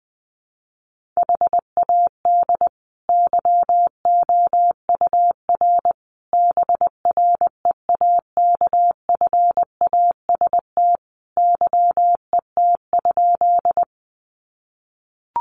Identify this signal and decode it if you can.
Morse code, heard as a keyed tone, about 20 wpm, 700 Hz. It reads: HAD YOUR BREAKFAST YET?
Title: Sentences from Top 1500 Words 20wpm